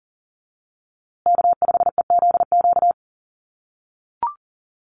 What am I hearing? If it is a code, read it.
Morse code, heard as a keyed tone, about 40 wpm, 700 Hz. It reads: K5EZQ
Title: USA Call Signs Repeated 2x 40wpm